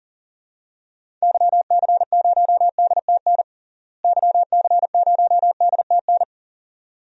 Morse code, heard as a keyed tone, about 40 wpm, 700 Hz. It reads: YC0DTN YC0DTN
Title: DXCC-100 Call Signs - 6 Characters - Repeated 2x - 40wpm